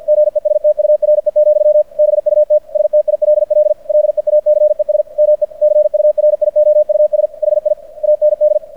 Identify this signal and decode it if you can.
Morse code, heard as a keyed tone, about 50 wpm, 600 Hz. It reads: GESTURE, BUT STILL LEACES ME CURIOUS HI AND
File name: archived / CRSnet